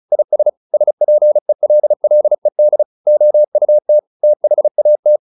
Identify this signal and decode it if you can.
Morse code, heard as a keyed tone, about 34 wpm, 600 Hz. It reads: IS SPELLED OUT THAT